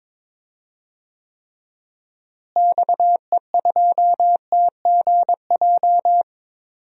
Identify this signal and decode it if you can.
Morse code, heard as a keyed tone, about 22 wpm, 700 Hz. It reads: XE2TGJ
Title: DXCC-100 Call Signs - 6 Characters - Repeated 2x - 22wpm